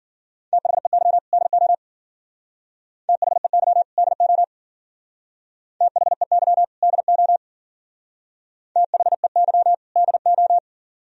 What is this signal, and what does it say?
Morse code, heard as a keyed tone, about 51 wpm, 700 Hz. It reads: THEY DO THEY DO THEY DO THEY DO